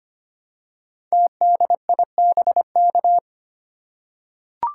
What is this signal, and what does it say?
Morse code, heard as a keyed tone, about 25 wpm, 700 Hz. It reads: TDIBK
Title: Code Groups - 5 Letters - 25wpm